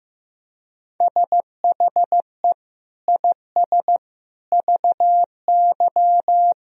Morse code, heard as a keyed tone, about 15 wpm, 700 Hz. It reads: SHE IS VY
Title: Sentences from Top 400 Words and Abbreviations 15wpm